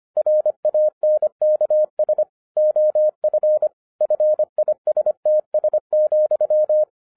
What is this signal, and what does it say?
Morse code, heard as a keyed tone, about 25 wpm, 610 Hz. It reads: RANKS OF FISTS,
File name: 2012-01-mp3 / n14-FUN WITH NUMBERS-25wpm